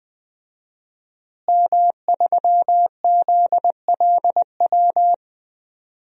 Morse code, heard as a keyed tone, about 20 wpm, 700 Hz. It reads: M3ZLW
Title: DXCC-100 Call Signs - 5 Characters - Repeated 2x - 20wpm